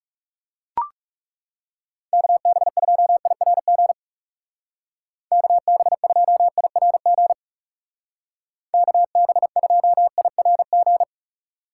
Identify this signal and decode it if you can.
Morse code, heard as a keyed tone, about 37 wpm, 700 Hz. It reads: KB2IRG KB2IRG KB2IRG
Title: Speed Racing - USA 2x3 Call Signs 30wpm